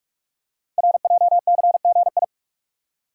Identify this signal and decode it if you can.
Morse code, heard as a keyed tone, about 48 wpm, 700 Hz. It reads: RJCGI
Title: Code Groups - 5 Letters - 45wpm